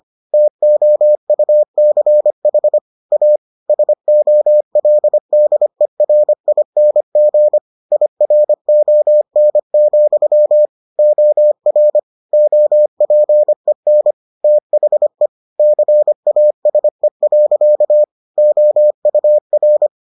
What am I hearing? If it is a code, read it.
Morse code, heard as a keyed tone, about 25 wpm, 600 Hz. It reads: TOUCH A SOLDERING IRON, OR OPEN THE CASE. OUR